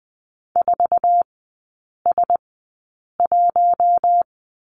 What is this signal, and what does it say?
Morse code, heard as a keyed tone, about 20 wpm, spaced as for 8 wpm, 700 Hz. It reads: 4S1
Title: 053 Lesson 10e - Letter r - Three Characters 20wpm